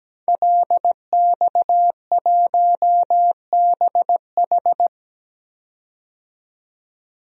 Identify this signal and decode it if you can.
Morse code, heard as a keyed tone, about 17 wpm, 700 Hz. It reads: LX1BH
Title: DXCC-100 Call Signs - 5 Characters - 17wpm